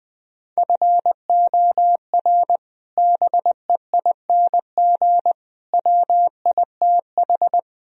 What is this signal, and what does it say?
Morse code, heard as a keyed tone, about 20 wpm, 700 Hz. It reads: FOR BEING WITH
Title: Speed-Racing - Sets of 4 Words 17wpm